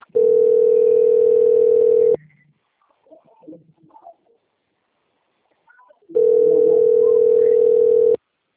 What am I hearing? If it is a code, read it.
Ringback tone (North America), 440 + 480 Hz.